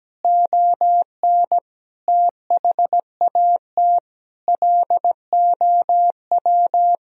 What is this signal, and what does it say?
Morse code, heard as a keyed tone, about 17 wpm, 700 Hz. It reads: ON THAT LOW
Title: Sentences from Top 1000 Words - Review Part 1 - 17wpm